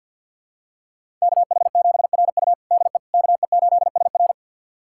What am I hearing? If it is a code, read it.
Morse code, heard as a keyed tone, about 53 wpm, 700 Hz. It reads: KH7RU DE KE8SR
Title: USA Callsign DE Callsign - 5 Characters - 50wpm